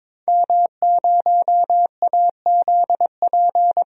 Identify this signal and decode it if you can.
Morse code, heard as a keyed tone, about 22 wpm, 700 Hz. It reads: M0AZP
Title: DXCC-100 Call Signs - 5 Characters - 22wpm